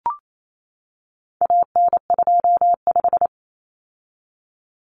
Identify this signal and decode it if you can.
Morse code, heard as a keyed tone, about 28 wpm, 700 Hz. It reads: AN25
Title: Common Cut Numbers - 4-Digit 28wpm